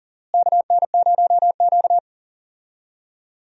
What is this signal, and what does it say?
Morse code, heard as a keyed tone, about 40 wpm, 700 Hz. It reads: KN0Q
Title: USA Call Signs - 4 Characters - Repeated 3x - 40wpm